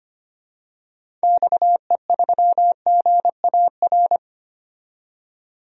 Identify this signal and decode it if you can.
Morse code, heard as a keyed tone, about 25 wpm, 700 Hz. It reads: XE3GAR
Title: DXCC-100 Call Signs - 6 Characters - 25wpm